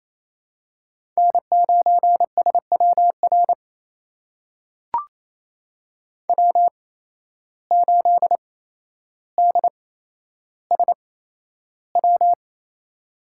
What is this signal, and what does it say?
Morse code, heard as a keyed tone, about 28 wpm, spaced as for 12 wpm, 700 Hz. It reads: N9SWR W8DSW
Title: USA Call Signs - 5 Characters - 8x Character Spacing - 28wpm